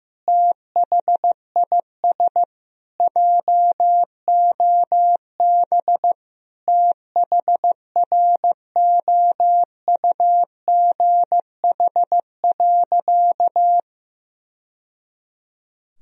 Morse code, heard as a keyed tone, about 15 wpm, 700 Hz. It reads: THIS JOB THROUGH.
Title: Sentences from Top 1000 Words - Review Part 3 - 15wpm